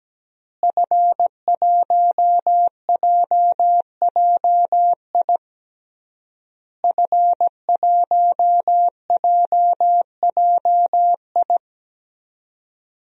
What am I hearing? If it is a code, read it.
Morse code, heard as a keyed tone, about 17 wpm, 700 Hz. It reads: F1JJI F1JJI
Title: DXCC-100 Call Signs - 5 Characters - Repeated 3x - 17wpm